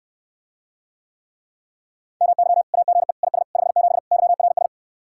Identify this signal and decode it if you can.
Morse code, heard as a keyed tone, about 69 wpm, 700 Hz. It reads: MY AGE IS 57 YRS